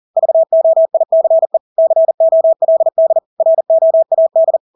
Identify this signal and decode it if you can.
Morse code, heard as a keyed tone, about 40 wpm, 650 Hz. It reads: VOICE COLD ROAD
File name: Engelske ord 40-40 wpm